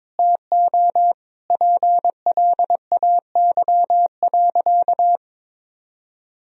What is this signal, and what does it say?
Morse code, heard as a keyed tone, about 21 wpm, 700 Hz. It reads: TO PLAY.
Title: Sentences from Top 1000 Words - Review Part 1 - 22wpm